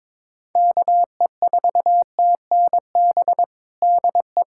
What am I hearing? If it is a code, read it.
Morse code, heard as a keyed tone, about 22 wpm, 700 Hz. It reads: KE4TNB DE
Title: Practice QSO - Exchange 4 - Ending - 22wpm